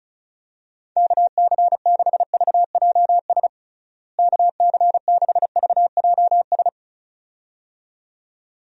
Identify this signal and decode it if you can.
Morse code, heard as a keyed tone, about 36 wpm, 700 Hz. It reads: KC6VJS KC6VJS
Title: USA Call Signs - 6 Characters - Repeated 2x - 35wpm